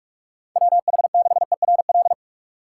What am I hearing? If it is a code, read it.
Morse code, heard as a keyed tone, about 43 wpm, 700 Hz. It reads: WH6ERL